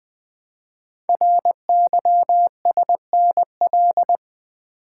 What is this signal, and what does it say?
Morse code, heard as a keyed tone, about 20 wpm, 700 Hz. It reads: RYSNL
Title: Code Groups - 5 Letters - 20wpm